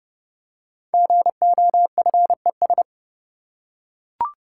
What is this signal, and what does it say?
Morse code, heard as a keyed tone, about 30 wpm, 700 Hz. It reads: GOFES
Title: Code Groups - 5 Letters - 30wpm